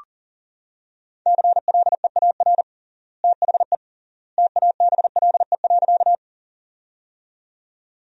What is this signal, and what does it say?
Morse code, heard as a keyed tone, about 37 wpm, 700 Hz. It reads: CLEAR THE TABLE.
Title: 3x Word-Spacing - Sentences from Top 500 Words - 40wpm